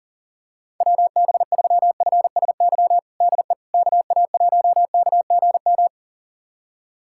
Morse code, heard as a keyed tone, about 40 wpm, 700 Hz. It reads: WB3FSY DE KA1KGM
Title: USA Callsign DE Callsign - 6 Characters - 40wpm